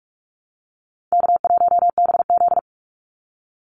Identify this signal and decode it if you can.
Morse code, heard as a keyed tone, about 44 wpm, 700 Hz. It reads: K1BZ
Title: Easy Call Signs Repeated 2x - 45wpm